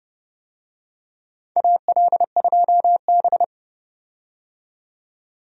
Morse code, heard as a keyed tone, about 30 wpm, 700 Hz. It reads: AL2B
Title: USA Call Signs - 4 Characters - 30wpm